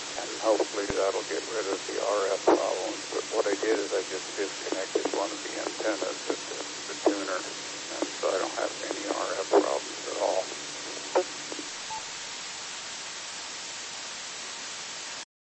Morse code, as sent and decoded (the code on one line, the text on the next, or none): none